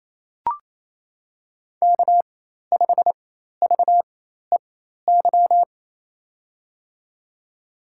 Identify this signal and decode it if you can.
Morse code, heard as a keyed tone, about 28 wpm, spaced as for 13 wpm, 700 Hz. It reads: K5VEY
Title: USA Call Signs - 5 Characters - 4x Character Spacing - 28wpm